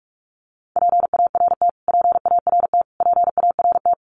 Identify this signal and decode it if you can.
Morse code, heard as a keyed tone, about 44 wpm, 700 Hz. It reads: PART PART PART
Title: Top 300 Words Repeated 45wpm